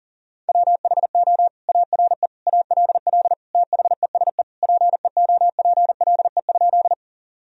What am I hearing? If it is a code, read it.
Morse code, heard as a keyed tone, about 37 wpm, 700 Hz. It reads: WHO ARE ALL THESE PEOPLE?
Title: Sentences from Top 1000 Words - Review Part 1 - 40wpm